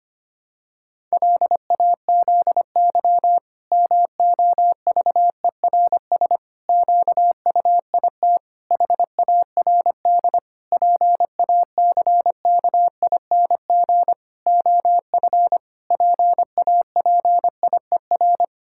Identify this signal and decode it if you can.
Morse code, heard as a keyed tone, about 25 wpm, 700 Hz. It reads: LAZY MOVERS QUIT HARD PACKING OF PAPIER
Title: Pangram Challenge v2 - 25wpm